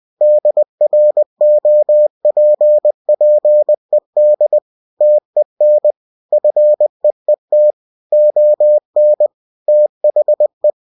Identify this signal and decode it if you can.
Morse code, heard as a keyed tone, about 20 wpm, 600 Hz. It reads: DROPPED TEN FEET ON THE
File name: cw-books / winnie-the-pooh / winnie-20wpm-page-0003